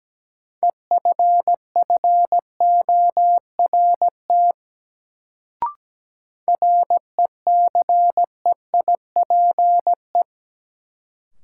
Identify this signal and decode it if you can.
Morse code, heard as a keyed tone, about 17 wpm, 700 Hz. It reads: EFFORT RECEIPE
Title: Sentences from Top 1200 Words - New Word Drill Alt - 17wpm